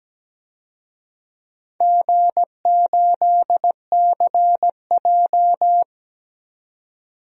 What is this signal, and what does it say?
Morse code, heard as a keyed tone, about 17 wpm, 700 Hz. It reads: G8CJ
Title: DXCC-100 Call Signs - 4 Characters - 17wpm